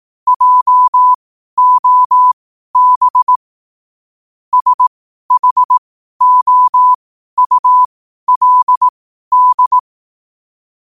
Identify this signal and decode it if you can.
Morse code, heard as a keyed tone, about 18 wpm, spaced as for 12 wpm, 1000 Hz. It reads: JOB SHOULD
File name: qotdfiles / 12wpm